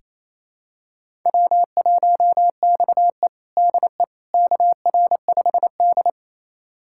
Morse code, heard as a keyed tone, about 28 wpm, 700 Hz. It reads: W1XE DE KR5D
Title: USA Callsign DE Callsign - 4 Characters - 28wpm